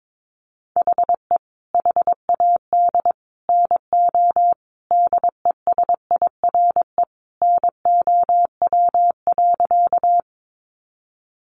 Morse code, heard as a keyed tone, about 22 wpm, 700 Hz. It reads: HE HAD NO DESIRE NOW.
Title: Sentences from Top 1100 Words 22wpm